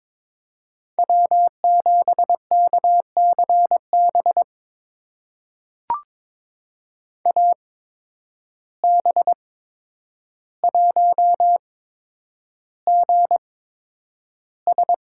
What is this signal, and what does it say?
Morse code, heard as a keyed tone, about 21 wpm, spaced as for 9 wpm, 700 Hz. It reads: W7KCB AB1GS
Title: USA Call Signs - 8x Character Spacing - 22wpm